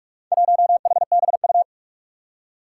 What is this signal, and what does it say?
Morse code, heard as a keyed tone, about 44 wpm, 700 Hz. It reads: 1HBU
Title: Four Challenging Letter-Number Combinations 45wpm